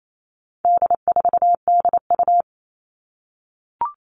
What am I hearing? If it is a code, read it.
Morse code, heard as a keyed tone, about 28 wpm, 700 Hz. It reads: D4DU
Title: All Cut Numbers - 4-Digit 28wpm